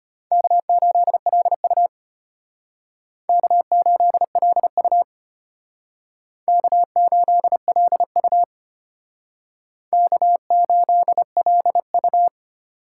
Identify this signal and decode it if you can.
Morse code, heard as a keyed tone, about 32 wpm, 700 Hz. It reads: K8LU K8LU K8LU K8LU